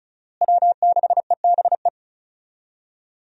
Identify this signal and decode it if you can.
Morse code, heard as a keyed tone, about 34 wpm, 700 Hz. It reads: W6EBE
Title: USA Call Signs - 5 Characters - Repeated 2x - 35wpm